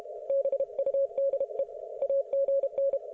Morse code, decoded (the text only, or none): BUDE AGN